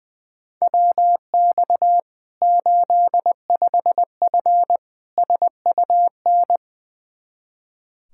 Morse code, heard as a keyed tone, about 20 wpm, 700 Hz. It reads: WX 85F SUN
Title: Practice QSO - Exchange 2 - Info - 20wpm